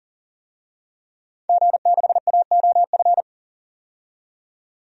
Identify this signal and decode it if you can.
Morse code, heard as a keyed tone, about 40 wpm, 700 Hz. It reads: G6AOF
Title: DXCC-100 Call Signs - 5 Characters - 40wpm